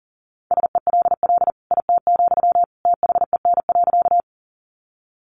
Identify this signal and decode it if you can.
Morse code, heard as a keyed tone, about 40 wpm, 700 Hz. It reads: SELL IT, THEN.
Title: Sentences from Top 1100 Words 40wpm